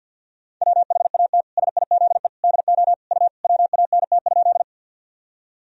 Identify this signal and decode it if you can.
Morse code, heard as a keyed tone, about 50 wpm, 700 Hz. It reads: WHAT SIZE DO U WANT?
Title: Sentences from Top 400 Words and Abbreviations 50wpm